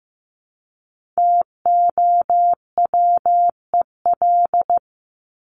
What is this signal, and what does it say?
Morse code, heard as a keyed tone, about 15 wpm, 700 Hz. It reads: TOWEL
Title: Top 1501-1750 Words 15wpm